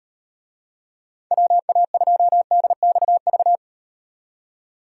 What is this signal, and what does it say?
Morse code, heard as a keyed tone, about 38 wpm, 700 Hz. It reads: WA2DXV